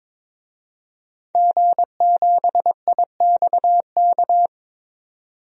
Morse code, heard as a keyed tone, about 21 wpm, 700 Hz. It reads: G7IXK
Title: DXCC-100 Call Signs - 5 Characters - 22wpm